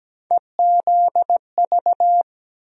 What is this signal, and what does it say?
Morse code, heard as a keyed tone, about 17 wpm, 700 Hz. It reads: EZV
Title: USA Call Sign Postfixes - 3 Letters 17wpm